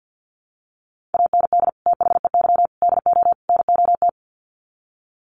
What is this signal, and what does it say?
Morse code, heard as a keyed tone, about 50 wpm, 700 Hz. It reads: AND THEY DO NOT
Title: Sets of 4 Words 50wpm